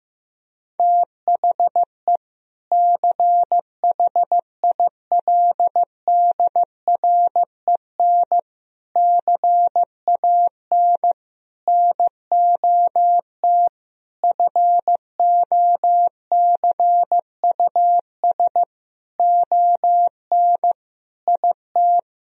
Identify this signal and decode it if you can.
Morse code, heard as a keyed tone, about 15 wpm, 700 Hz. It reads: THE CHILDREN CAN NOT FOCUS ON IT
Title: Sentences from Top 1000 Words - Review Part 4 - 15wpm